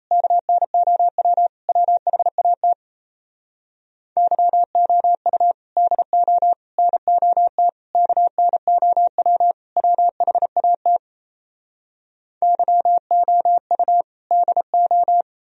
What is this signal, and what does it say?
Morse code, heard as a keyed tone, about 37 wpm, 700 Hz. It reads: KNOW WHAT YOU DO NOT KNOW WHAT YOU DO